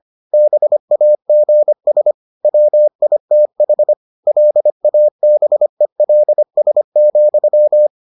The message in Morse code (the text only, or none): BAGS WITH LABELS,